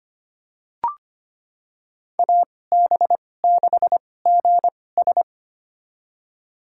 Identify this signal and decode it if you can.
Morse code, heard as a keyed tone, about 25 wpm, spaced as for 18 wpm, 700 Hz. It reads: AB6GS
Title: USA Call Signs - 5 Characters - 2x Character Spacing - 25wpm